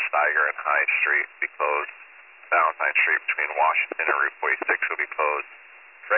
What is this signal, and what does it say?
Emergency Alert System (SAME) tones, heard as harsh buzzing bursts, with speech. It is heard from 1.1 to 5.1 s.